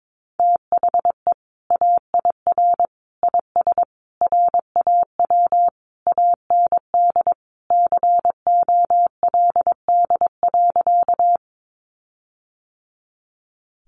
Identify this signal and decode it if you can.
Morse code, heard as a keyed tone, about 22 wpm, 700 Hz. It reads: THE AIR IS RAW AND COLD.
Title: Sentences from Top 1500 Words 22wpm